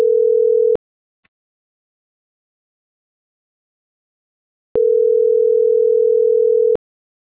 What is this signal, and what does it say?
Ringback tone (North America), 440 + 480 Hz.